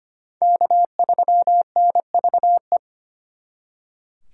Morse code, heard as a keyed tone, about 32 wpm, 700 Hz. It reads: K3NVE